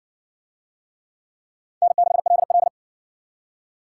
Morse code, heard as a keyed tone, about 52 wpm, 700 Hz. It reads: N6LL